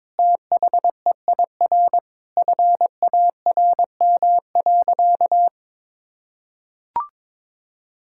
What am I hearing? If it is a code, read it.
Morse code, heard as a keyed tone, about 22 wpm, 700 Hz. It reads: THEIR FARM.
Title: Sentences from Top 1000 Words - Review Part 6 - 22wpm